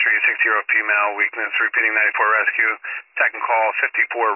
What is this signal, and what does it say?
Emergency Alert System (SAME) tones, heard as harsh buzzing bursts, with speech. It is heard from 1.1 to 4.1 s.